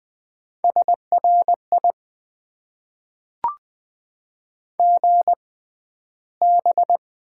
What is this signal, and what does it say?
Morse code, heard as a keyed tone, about 20 wpm, spaced as for 12 wpm, 700 Hz. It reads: SRI GB
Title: Top 90 QSO Elements - 6x Character Spacing - 20wpm